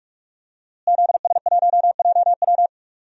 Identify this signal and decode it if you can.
Morse code, heard as a keyed tone, about 44 wpm, 700 Hz. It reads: ZS1JW
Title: DXCC-100 Call Signs - 5 Characters - Repeated 3x - 45wpm